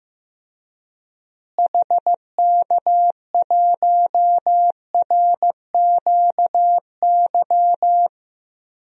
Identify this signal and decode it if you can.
Morse code, heard as a keyed tone, about 15 wpm, 700 Hz. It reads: HK1RQY
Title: DXCC-100 Call Signs - 6 Characters - Repeated 2x - 15wpm